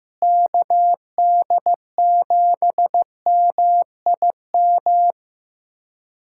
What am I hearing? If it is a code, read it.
Morse code, heard as a keyed tone, about 15 wpm, 700 Hz. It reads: KD7MIM
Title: USA Callsign DE Callsign - 6 Characters - 15wpm